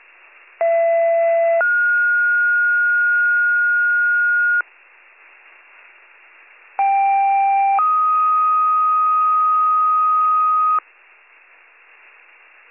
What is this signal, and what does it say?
Two-tone page, heard as a one-second tone, then a three-second tone, about 670 then 1430 Hz, about 790 then 1250 Hz. It is heard from 0.6 to 10.8 s.